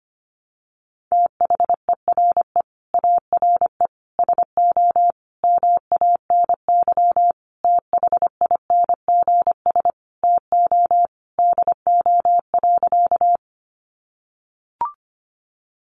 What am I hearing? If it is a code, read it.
Morse code, heard as a keyed tone, about 25 wpm, 700 Hz. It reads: THERE ARE SO MANY THINGS TO DO.
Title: Additional Easy Phrases 25wpm